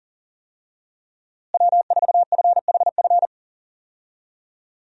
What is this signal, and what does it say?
Morse code, heard as a keyed tone, about 40 wpm, 700 Hz. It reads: W4FHF
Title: USA Call Signs - 5 Characters - Repeated 3x - 40wpm